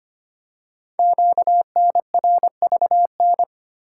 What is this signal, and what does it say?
Morse code, heard as a keyed tone, about 25 wpm, 700 Hz. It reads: QNRVN
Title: Code Groups - 5 Letters - 25wpm